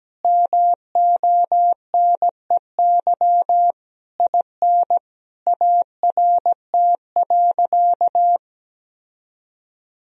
Morse code, heard as a keyed tone, about 17 wpm, 700 Hz. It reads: MONEY IN ART.
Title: Sentences from Top 1000 Words - Review Part 3 - 17wpm